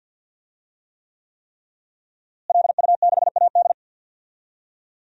Morse code, heard as a keyed tone, about 52 wpm, 700 Hz. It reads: LU6AD
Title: DXCC-100 Call Signs - 5 Characters - Repeated 3x - 50wpm